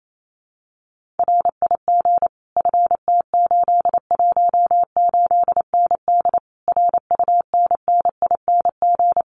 Morse code, heard as a keyed tone, about 28 wpm, 700 Hz. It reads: RIG FT818ND RUNNING
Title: Common QSO Phrases Encore 28wpm